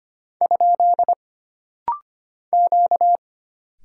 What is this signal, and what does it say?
Morse code, heard as a keyed tone, about 25 wpm, 700 Hz. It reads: ? Q
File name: Single Character (Letter:Number:Punctuation:Prosign) - Rapid-Fire - 25wpm